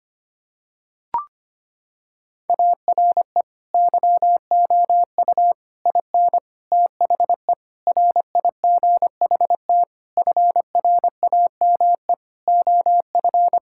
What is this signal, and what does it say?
Morse code, heard as a keyed tone, about 25 wpm, 700 Hz. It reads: ARE YOU IN THE RIGHT FRAME OF
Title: Sentences from Top 1000 Words - Review Part 6 - 25wpm